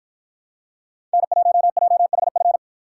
Morse code, heard as a keyed tone, about 53 wpm, 700 Hz. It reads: N1JHF